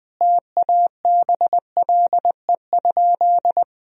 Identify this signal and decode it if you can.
Morse code, heard as a keyed tone, about 20 wpm, 700 Hz. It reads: TABLE?
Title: Sentences from Top 1000 Words - Review Part 2 - 20wpm